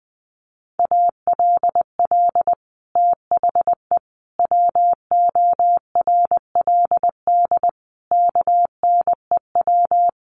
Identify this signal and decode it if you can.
Morse code, heard as a keyed tone, about 20 wpm, 700 Hz. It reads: ALL THE WORLD KNEW
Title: Sentences from Top 1200 Words 20wpm